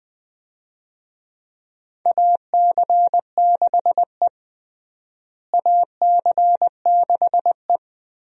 Morse code, heard as a keyed tone, about 21 wpm, 700 Hz. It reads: AC6E AC6E
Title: USA Call Signs - 4 Characters - Repeated 2x - 20wpm